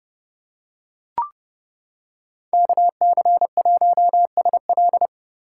Morse code, heard as a keyed tone, about 30 wpm, 700 Hz. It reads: KC1SL